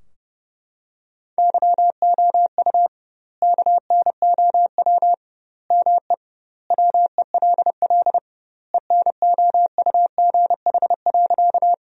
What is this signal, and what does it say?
Morse code, heard as a keyed tone, about 32 wpm, 700 Hz. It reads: YOU KNOW ME WELL ENOUGH.
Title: 2x Word-Spacing - Sentences from Top 400 Words - 30wpm